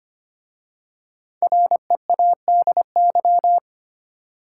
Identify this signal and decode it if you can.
Morse code, heard as a keyed tone, about 25 wpm, 700 Hz. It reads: READY
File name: 5 Letter Words - First 3 Letters Given - 25wpm